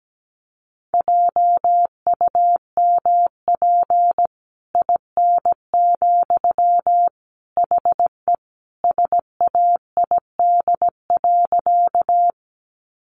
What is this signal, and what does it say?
Morse code, heard as a keyed tone, about 17 wpm, 700 Hz. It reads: JUMP IN, HE SAID.
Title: Sentences from Top 1300 Words 17wpm